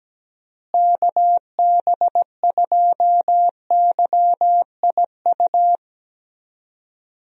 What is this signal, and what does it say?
Morse code, heard as a keyed tone, about 17 wpm, 700 Hz. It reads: KB2YIU
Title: USA Call Signs - 6 Characters - Repeated 3x - 17wpm